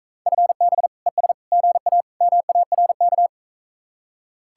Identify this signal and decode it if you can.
Morse code, heard as a keyed tone, about 42 wpm, 700 Hz. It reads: FB ES GA MARK